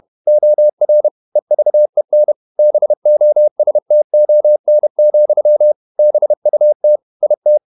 Morse code, heard as a keyed tone, about 31 wpm, 600 Hz. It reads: OR EVEN BOSTON, BUT IT
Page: Never 31wpm | CW med Gnister